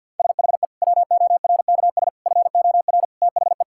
Morse code, heard as a keyed tone, about 50 wpm, 700 Hz. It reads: SHE WORKS FOR THE
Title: Sentences from Top 1000 Words - Review Part 5 - 50wpm